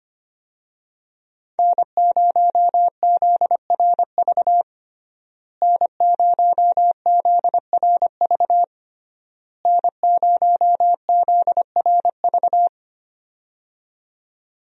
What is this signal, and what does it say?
Morse code, heard as a keyed tone, about 25 wpm, 700 Hz. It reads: N0ZRV N0ZRV N0ZRV
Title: USA Call Signs - 5 Characters - Repeated 3x - 25wpm